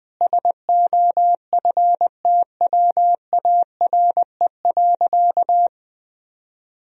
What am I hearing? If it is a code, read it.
Morse code, heard as a keyed tone, about 20 wpm, 700 Hz. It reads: SOFTWARE.
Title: Sentences from Top 1000 Words - Review Part 3 - 20wpm